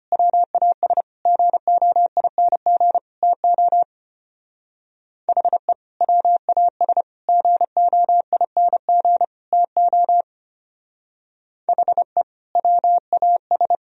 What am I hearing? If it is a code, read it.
Morse code, heard as a keyed tone, about 32 wpm, 700 Hz. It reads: WAS GOING TO HE WAS GOING TO HE WAS